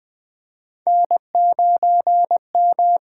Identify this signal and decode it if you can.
Morse code, heard as a keyed tone, about 20 wpm, 700 Hz. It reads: N9M
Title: USA Call Signs - 5 Characters - 8x Character Spacing - 20wpm